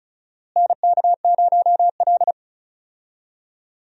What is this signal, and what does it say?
Morse code, heard as a keyed tone, about 34 wpm, 700 Hz. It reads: NK0L
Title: USA Call Signs - 4 Characters - Repeated 3x - 35wpm